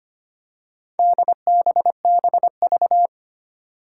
Morse code, heard as a keyed tone, about 25 wpm, 700 Hz. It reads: DBBV
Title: Four Challenging Letter-Number Combinations 25wpm